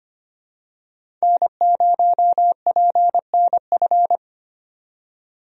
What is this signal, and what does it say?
Morse code, heard as a keyed tone, about 25 wpm, 700 Hz. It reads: N0PNF
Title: USA Call Signs - 5 Characters - 25wpm